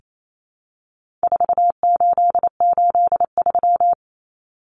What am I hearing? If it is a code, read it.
Morse code, heard as a keyed tone, about 28 wpm, 700 Hz. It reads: 4883
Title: Numbers - 4-Digit 28wpm